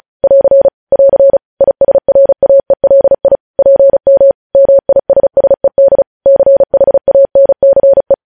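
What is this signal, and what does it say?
Morse code, heard as a keyed tone, about 36 wpm, 550 Hz. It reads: <AR> <AR> ISRAELI PM MISSED CHANCE
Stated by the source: Morse news headlines practice at 35.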